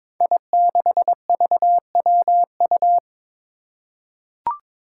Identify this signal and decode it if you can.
Morse code, heard as a keyed tone, about 21 wpm, 700 Hz. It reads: I6VWU
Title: Five Challenging Letter-Number Combinations 22wpm